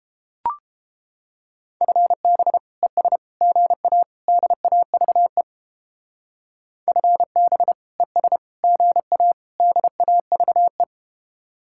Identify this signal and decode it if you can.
Morse code, heard as a keyed tone, about 31 wpm, 700 Hz. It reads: FB ES GA DAVE FB ES GA DAVE